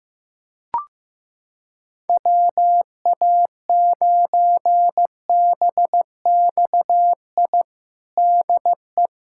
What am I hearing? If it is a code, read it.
Morse code, heard as a keyed tone, about 15 wpm, 700 Hz. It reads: WA9BXI DE
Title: K1USN SST QSO Practice 15wpm